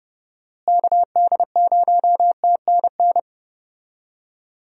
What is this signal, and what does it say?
Morse code, heard as a keyed tone, about 30 wpm, 700 Hz. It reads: KD0TNN
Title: USA Call Signs - 6 Characters - Repeated 3x - 30wpm